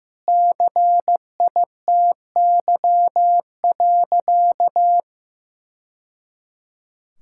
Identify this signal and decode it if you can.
Morse code, heard as a keyed tone, about 15 wpm, 700 Hz. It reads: CITY.
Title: Sentences from Top 1000 Words - Review Part 5 - 15wpm